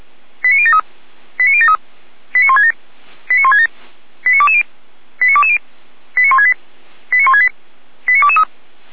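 Five-tone selective call (ZVEI), heard as quick runs of five tones, tones of about 70 ms. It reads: 89073 89073 89167 89167 89220 89220 89168 89168 89223
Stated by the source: ZVEI five-tone selective call 89001 received by SDR, from audio server of Feuerwehr Orlamünde (feuerwehr-orlamuende).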